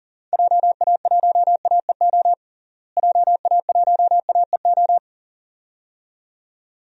Morse code, heard as a keyed tone, about 40 wpm, 700 Hz. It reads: JA1AEO JA1AEO
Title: DXCC-100 Call Signs - 6 Characters - Repeated 2x - 40wpm